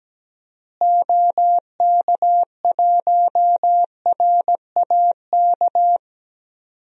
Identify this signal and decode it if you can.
Morse code, heard as a keyed tone, about 17 wpm, 700 Hz. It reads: OK1RAK
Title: DXCC-100 Call Signs - 6 Characters - Repeated 3x - 17wpm